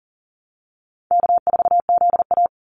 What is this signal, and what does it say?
Morse code, heard as a keyed tone, about 39 wpm, 700 Hz. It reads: K4ZA
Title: Easy Call Signs Repeated 3x - 40wpm